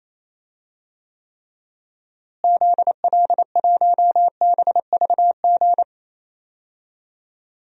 Morse code, heard as a keyed tone, about 28 wpm, 700 Hz. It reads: ZL1BVG
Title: DXCC-100 Call Signs - 6 Characters - Repeated 3x - 28wpm